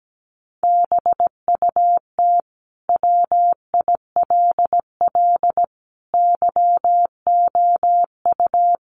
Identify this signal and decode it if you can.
Morse code, heard as a keyed tone, about 17 wpm, 700 Hz. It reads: BUT WILL YOU
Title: Sentences from Top 1400 Words 17wpm